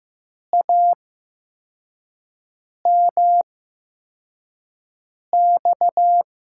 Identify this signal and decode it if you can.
Morse code, heard as a keyed tone, about 15 wpm, spaced as for 4 wpm, 700 Hz. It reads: AMX
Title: USA Call Signs - 5 Characters - 8x Character Spacing - 15wpm